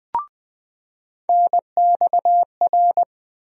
Morse code, heard as a keyed tone, about 20 wpm, 700 Hz. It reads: NXR
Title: USA Call Sign Postfixes - 3 Letters 20wpm